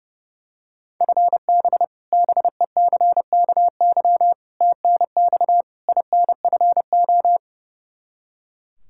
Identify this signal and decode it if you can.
Morse code, heard as a keyed tone, about 32 wpm, 700 Hz. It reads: FB BECKY TNX INFO
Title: Practice QSO - Exchange 3 - More Info - 30wpm